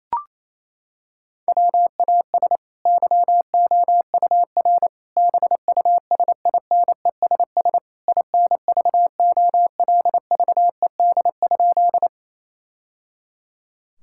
Morse code, heard as a keyed tone, about 28 wpm, 700 Hz. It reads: WAS YOUR BUSINESS INVOLVED?
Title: Sentences from Top 1000 Words - Review Part 5 - 28wpm